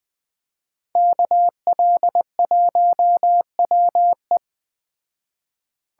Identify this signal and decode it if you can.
Morse code, heard as a keyed tone, about 20 wpm, 700 Hz. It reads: KL1WE
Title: Speed Racing - USA 2x2 Call Signs 17wpm